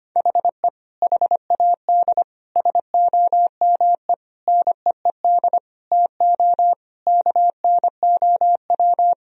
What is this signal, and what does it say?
Morse code, heard as a keyed tone, about 25 wpm, 700 Hz. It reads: HE HAD SOME NEED TO KNOW
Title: Sentences from Top 1000 Words - Review Part 3 - 25wpm